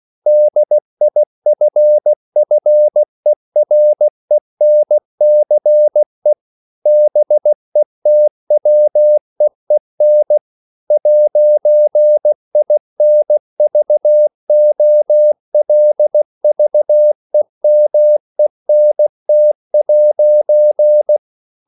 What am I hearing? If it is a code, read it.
Morse code, heard as a keyed tone, about 16 wpm, 600 Hz. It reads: DIFFERENCE BETWEEN 'INVOLVEMENT'
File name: CW / Practice / EnglishQuotations0116